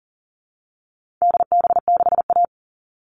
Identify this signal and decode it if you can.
Morse code, heard as a keyed tone, about 40 wpm, 700 Hz. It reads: DB6A
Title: All Cut Numbers - 4-Digit 40wpm